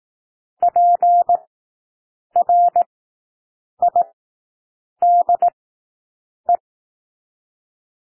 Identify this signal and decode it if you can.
Morse code, heard as a keyed tone, about 18 wpm, spaced as for 7 wpm, 700 Hz. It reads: PRIDE
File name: SOUND WORDS 18 wpm